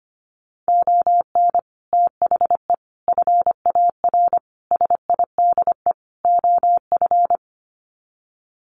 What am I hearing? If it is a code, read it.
Morse code, heard as a keyed tone, about 25 wpm, 700 Hz. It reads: ON THE FAR SIDE OF
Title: Sets of 5 Words 25wpm